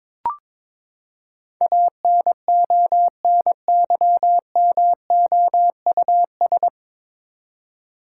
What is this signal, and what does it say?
Morse code, heard as a keyed tone, about 22 wpm, 700 Hz. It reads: ANONYMOUS
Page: Morse Code Ninja practice 22wpm